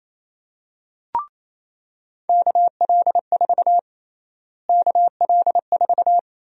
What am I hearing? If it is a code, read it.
Morse code, heard as a keyed tone, about 28 wpm, 700 Hz. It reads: KL4 KL4
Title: USA Call Sign Prefixes - Repeated 2x - 28wpm